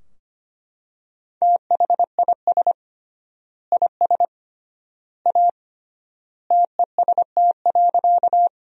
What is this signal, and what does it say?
Morse code, heard as a keyed tone, about 25 wpm, 700 Hz. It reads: THIS IS A TEST.
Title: 3x Word-Spacing - Sentences from Top 500 Words - 25wpm